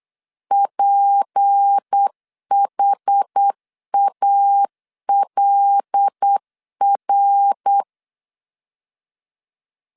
Morse code, heard as a keyed tone, about 8 wpm, 800 Hz. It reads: PHALR
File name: LEZIONI / 6 PKX0 / 6 P 8wpm 3025